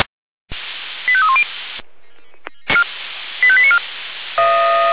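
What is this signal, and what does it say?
Five-tone selective call (ZVEI), heard as quick runs of five tones, tones of about 70 ms. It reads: 85310 85894
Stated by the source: Five-tone ZVEI call received on an SDR.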